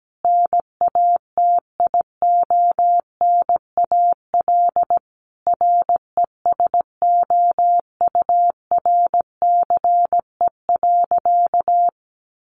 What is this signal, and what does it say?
Morse code, heard as a keyed tone, about 17 wpm, 700 Hz. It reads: NATIONAL RESOURCE.
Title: Sentences from Top 1300 Words 17wpm